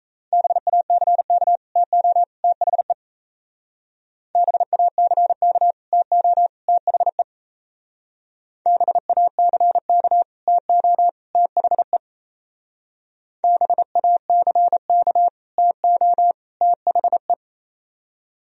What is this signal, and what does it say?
Morse code, heard as a keyed tone, about 34 wpm, 700 Hz. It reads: BACK TO THE BACK TO THE BACK TO THE BACK TO THE
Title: Speed-Racing - Sets of 3 Words 28wpm